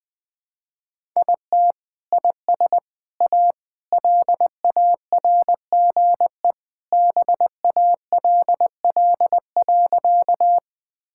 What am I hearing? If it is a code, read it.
Morse code, heard as a keyed tone, about 21 wpm, 700 Hz. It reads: IT IS A LARGE BALL.
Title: Sentences from Top 1000 Words - Review Part 6 - 20wpm